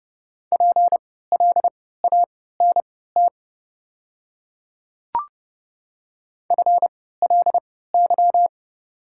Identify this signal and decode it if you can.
Morse code, heard as a keyed tone, about 30 wpm, spaced as for 17 wpm, 700 Hz. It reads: PLANT FLY
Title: 3x Character Spacing - Top 500 Words - 30wpm